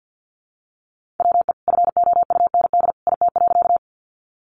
Morse code, heard as a keyed tone, about 48 wpm, 700 Hz. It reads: WE FOUND IT.
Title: Sentences from Top 300 Words 50wpm